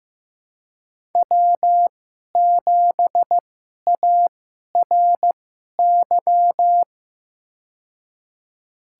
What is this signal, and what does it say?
Morse code, heard as a keyed tone, about 15 wpm, spaced as for 11 wpm, 700 Hz. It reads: W7ARY
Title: USA Call Signs - 5 Characters - 2x Character Spacing - 15wpm